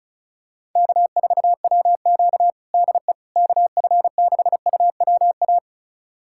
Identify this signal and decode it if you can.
Morse code, heard as a keyed tone, about 34 wpm, 700 Hz. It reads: K4WQ DE KF6UWA
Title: USA Callsign DE Callsign 35wpm